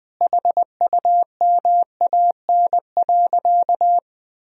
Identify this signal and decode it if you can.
Morse code, heard as a keyed tone, about 21 wpm, 700 Hz. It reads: HUMAN.
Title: Sentences from Top 1000 Words - Review Part 3 - 20wpm